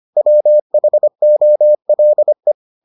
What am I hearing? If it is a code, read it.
Morse code, heard as a keyed tone, about 25 wpm, 600 Hz. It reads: WHOLE